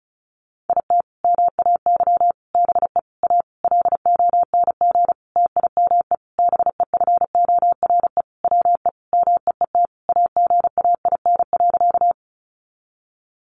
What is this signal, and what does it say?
Morse code, heard as a keyed tone, about 34 wpm, 700 Hz. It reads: IT MAY BE A LONG TIME BEFORE WE MEET AGAIN.
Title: Sentences from Top 1100 Words 35wpm